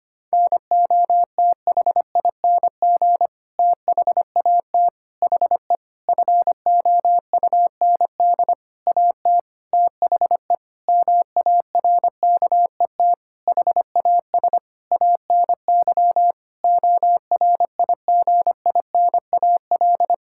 Morse code, heard as a keyed tone, about 25 wpm, 700 Hz. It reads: NOTHING THAT HE FOUND AT THE MARKET HAS ANY ORIGINAL
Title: Sentences from Top 1000 Words - Review Part 6 - 25wpm